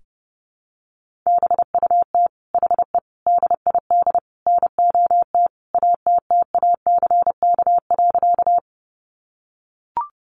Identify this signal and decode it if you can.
Morse code, heard as a keyed tone, about 30 wpm, 700 Hz. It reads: BUT HE DID NOT ATTACK.
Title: Sentences from Top 1500 Words 30wpm